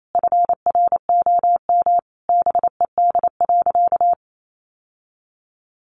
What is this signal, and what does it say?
Morse code, heard as a keyed tone, about 28 wpm, 700 Hz. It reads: FROM BED.
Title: Sentences from Top 900 Words 28wpm